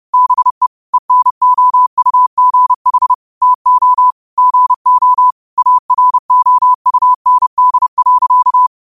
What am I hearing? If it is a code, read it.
Morse code, heard as a keyed tone, about 30 wpm, 1000 Hz. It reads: BE ENOUGH TO GO AROUND.
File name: qotdfiles / 30wpm